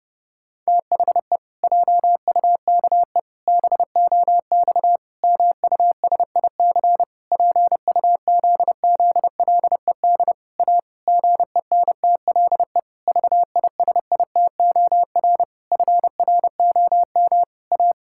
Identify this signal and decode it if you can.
Morse code, heard as a keyed tone, about 30 wpm, 700 Hz. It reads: THE JUKE BOX MUSIC PUZZLED A GENTLE VISITOR FROM A
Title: Pangram Challenge v2 - 30wpm